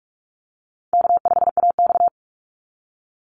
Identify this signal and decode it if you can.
Morse code, heard as a keyed tone, about 46 wpm, 700 Hz. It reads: K5AX
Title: Call Signs - 45wpm